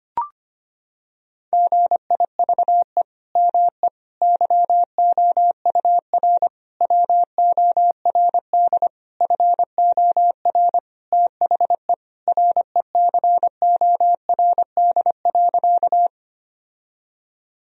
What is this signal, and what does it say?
Morse code, heard as a keyed tone, about 25 wpm, 700 Hz. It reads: GIVE ME YOUR WORD FOR THE RECORD.
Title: Sentences from Top 1000 Words - Review Part 3 - 25wpm